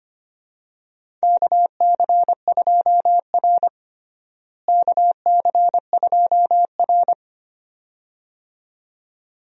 Morse code, heard as a keyed tone, about 25 wpm, 700 Hz. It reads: KC2R KC2R
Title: USA Call Signs - 4 Characters - Repeated 2x - 25wpm